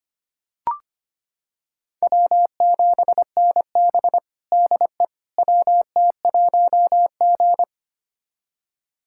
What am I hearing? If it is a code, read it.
Morse code, heard as a keyed tone, about 25 wpm, 700 Hz. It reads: W7NB DE WT1G
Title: USA Callsign DE Callsign - 4 Characters - 25wpm